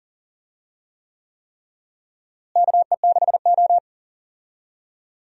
Morse code, heard as a keyed tone, about 40 wpm, 700 Hz. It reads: KE6O
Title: USA Call Signs - 4 Characters - Repeated 3x - 40wpm